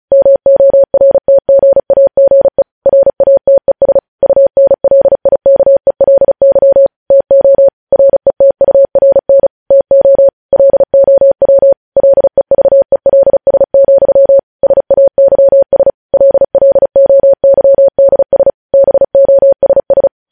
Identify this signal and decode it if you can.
Morse code, heard as a keyed tone, about 36 wpm, 550 Hz. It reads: MORTGAGE RATES UNLIKELY TO RETURN TO LOW LEVELS, SAYS LLOYDS BOSS